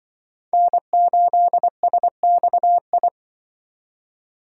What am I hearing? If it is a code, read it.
Morse code, heard as a keyed tone, about 24 wpm, 700 Hz. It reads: N8SXI
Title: Speed Racing - USA Call Signs 20wpm